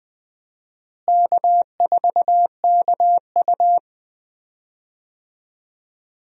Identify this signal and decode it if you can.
Morse code, heard as a keyed tone, about 20 wpm, 700 Hz. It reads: K4KU
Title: USA Call Signs - 4 Characters - Repeated 3x - 20wpm